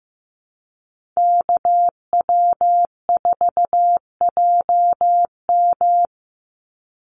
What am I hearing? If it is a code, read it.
Morse code, heard as a keyed tone, about 15 wpm, 700 Hz. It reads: KW4JM
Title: NA SOTA Activator Callsigns 15wpm